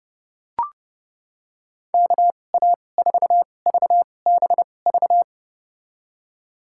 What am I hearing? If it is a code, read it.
Morse code, heard as a keyed tone, about 32 wpm, spaced as for 22 wpm, 700 Hz. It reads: KA4VBV
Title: USA Call Signs - 2x Character Spacing - 30wpm